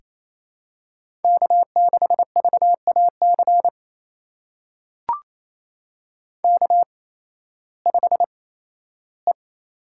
Morse code, heard as a keyed tone, about 28 wpm, spaced as for 15 wpm, 700 Hz. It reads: K6VAC K5E
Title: USA Call Signs - 5 Characters - 8x Character Spacing - 28wpm